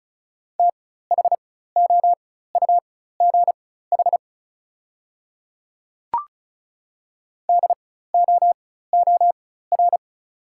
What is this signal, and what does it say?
Morse code, heard as a keyed tone, about 36 wpm, spaced as for 16 wpm, 690 Hz. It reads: THOUGH DOOR
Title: 4x Character Spacing - Top 500 Words - 35wpm